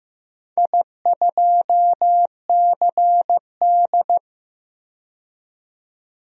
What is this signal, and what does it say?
Morse code, heard as a keyed tone, about 15 wpm, 700 Hz. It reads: I2CD
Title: DXCC-100 Call Signs - 4 Characters - 15wpm